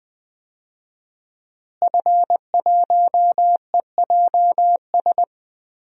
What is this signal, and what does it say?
Morse code, heard as a keyed tone, about 20 wpm, 700 Hz. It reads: F1EJS
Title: DXCC-100 Call Signs - 5 Characters - Repeated 3x - 20wpm